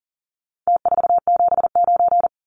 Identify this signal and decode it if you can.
Morse code, heard as a keyed tone, about 40 wpm, 700 Hz. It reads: T479
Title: Numbers - Contest Style - 4-Digit 40wpm